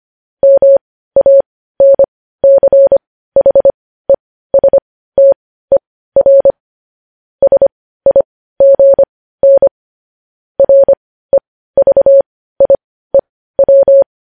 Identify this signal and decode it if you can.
Morse code, heard as a keyed tone, about 25 wpm, spaced as for 15 wpm, 550 Hz. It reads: MANCHESTER SIGN REVIEW
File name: podcast / News headlines-1x-25x15